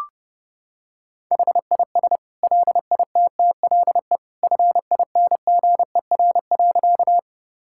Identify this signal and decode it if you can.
Morse code, heard as a keyed tone, about 32 wpm, 700 Hz. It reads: HIS LITTLE FINGER.
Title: Sentences from Top 1600 Words 30wpm